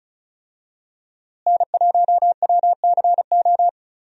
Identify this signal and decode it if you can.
Morse code, heard as a keyed tone, about 34 wpm, 700 Hz. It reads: N1WCO
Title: USA Call Signs - 5 Characters - Repeated 3x - 35wpm